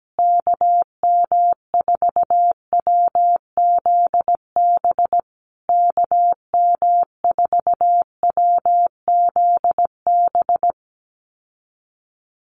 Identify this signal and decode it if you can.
Morse code, heard as a keyed tone, about 17 wpm, 700 Hz. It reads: KM4WZB KM4WZB
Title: USA Call Signs Repeated 3x 17wpm